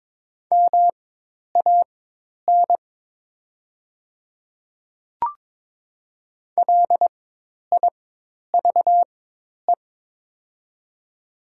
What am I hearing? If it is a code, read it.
Morse code, heard as a keyed tone, about 22 wpm, spaced as for 10 wpm, 700 Hz. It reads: MAN LIVE
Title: 4x Character Spacing - Top 200 Words - 22wpm